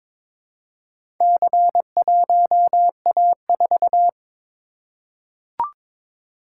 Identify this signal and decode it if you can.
Morse code, heard as a keyed tone, about 22 wpm, 700 Hz. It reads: C1A4
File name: practice / 22wpm / Four Letter-Number Combinations - 22wpm